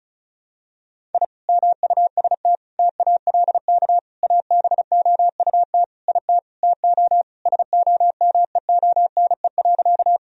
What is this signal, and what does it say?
Morse code, heard as a keyed tone, about 36 wpm, 700 Hz. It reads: I MUST TALK ABOUT IT TO SOMEONE.
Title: Sentences from Top 1000 Words - Review Part 4 - 35wpm